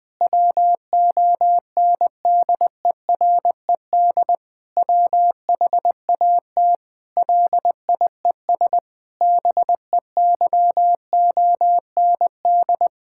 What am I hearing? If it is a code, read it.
Morse code, heard as a keyed tone, about 20 wpm, 700 Hz. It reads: WONDERED WHAT LIES BEYOND
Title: Difficult Sentences with Hints - 20wpm